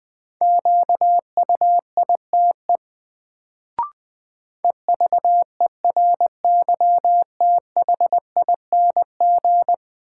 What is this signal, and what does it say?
Morse code, heard as a keyed tone, about 20 wpm, 700 Hz. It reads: QUITE EVERYTHING
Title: Sentences from Top 700 Words - New Word Drill - 20wpm